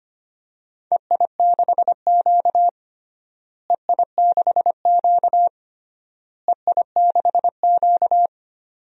Morse code, heard as a keyed tone, about 25 wpm, 700 Hz. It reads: EI6Q EI6Q EI6Q
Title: DXCC-100 Call Signs - 4 Characters - Repeated 3x - 25wpm